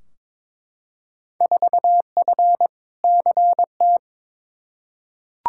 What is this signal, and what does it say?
Morse code, heard as a keyed tone, about 21 wpm, 700 Hz. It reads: 4F CT
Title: Field Day Exchanges Encore - 22wpm